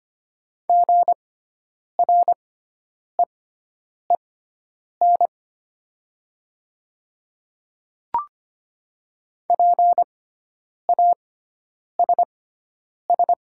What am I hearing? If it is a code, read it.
Morse code, heard as a keyed tone, about 25 wpm, spaced as for 9 wpm, 700 Hz. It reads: GREEN PASS
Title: 6x Character Spacing - Top 500 Words - 25wpm